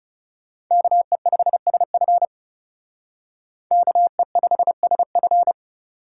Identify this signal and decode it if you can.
Morse code, heard as a keyed tone, about 36 wpm, 700 Hz. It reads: KE5SF KE5SF
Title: Speed Racing - USA 2x2 Call Signs 30wpm